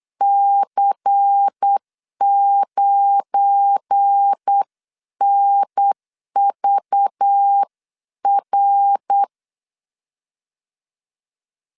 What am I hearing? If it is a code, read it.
Morse code, heard as a keyed tone, about 8 wpm, 800 Hz. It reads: C9NVR
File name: LEZIONI / 7 GF9Z / 7 GF9 8wpm 3545